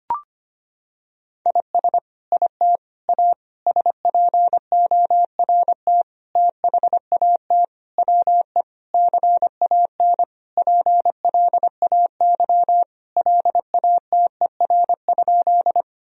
Morse code, heard as a keyed tone, about 25 wpm, 700 Hz. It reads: IS IT A SPORT THAT WE CAN PLAY LATER?
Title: Sentences from Top 1000 Words - Review Part 4 - 25wpm